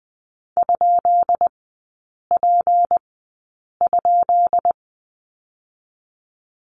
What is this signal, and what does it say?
Morse code, heard as a keyed tone, about 20 wpm, spaced as for 8 wpm, 700 Hz. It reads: ?P?
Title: 198 Lesson 32e - Letter k - Three Characters 20wpm